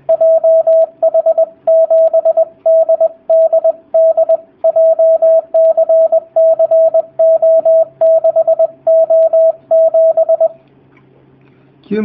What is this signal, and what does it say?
Morse code, heard as a keyed tone, about 20 wpm, 650 Hz, with radio noise. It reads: JH7DDD JCCO6O7
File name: jh8cbh / 2019cw / sp20- w